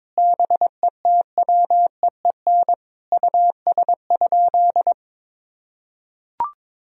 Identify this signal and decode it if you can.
Morse code, heard as a keyed tone, about 22 wpm, 700 Hz. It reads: BETWEEN US?
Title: Sentences from Top 1000 Words - Review Part 1 - 22wpm